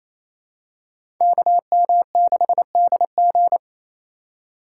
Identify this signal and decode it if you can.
Morse code, heard as a keyed tone, about 27 wpm, 700 Hz. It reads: KM6DG
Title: USA Call Signs - 5 Characters - 2x Character Spacing - 28wpm